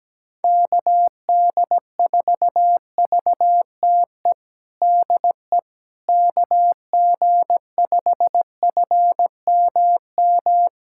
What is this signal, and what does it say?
Morse code, heard as a keyed tone, about 17 wpm, 700 Hz. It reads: KD4VTE DE KG5FMM
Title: USA Callsign DE Callsign - 6 Characters - 17wpm